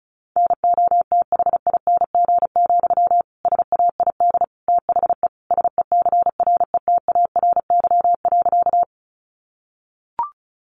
Morse code, heard as a keyed tone, about 34 wpm, 700 Hz. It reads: NOTHING, SAID THE SECRETARY.
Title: Sentences from Top 1300 Words 35wpm